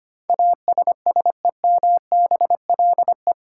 Morse code, heard as a keyed tone, about 25 wpm, 700 Hz. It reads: ASSEMBLE
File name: practice / 25wpm / 8 Letter Words - First 3 Letters Given - 25wpm